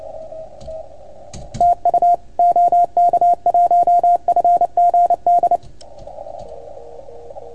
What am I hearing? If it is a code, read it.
Morse code, heard as a keyed tone, about 28 wpm, 690 Hz, with radio noise. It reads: TU OK1FGD
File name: Archiv / CUC310 / OK1FGD CUC 310 pileup